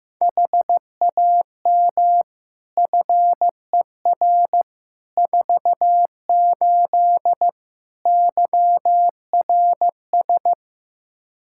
Morse code, heard as a keyed tone, about 15 wpm, 700 Hz. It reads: HAM FER 48 YRS
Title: Practice QSO - Exchange 3 - More Info - 15wpm